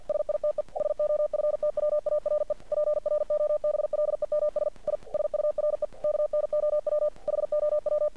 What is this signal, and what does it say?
Morse code, heard as a keyed tone, about 48 wpm, 600 Hz. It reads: SITE SOFTWARE PROBLEMS I SURE KNOW HOW
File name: archived / CRSnet